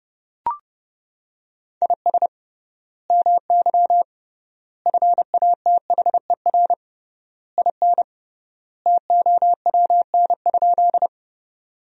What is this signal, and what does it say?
Morse code, heard as a keyed tone, about 30 wpm, 700 Hz. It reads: IS MY FATHER IN TOWN?
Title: 3x Word-Spacing - Sentences from Top 500 Words - 30wpm